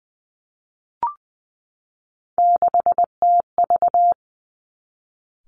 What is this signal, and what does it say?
Morse code, heard as a keyed tone, about 19 wpm, 700 Hz. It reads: 6TV
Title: Three Challenging Letter-Number Combinations 20wpm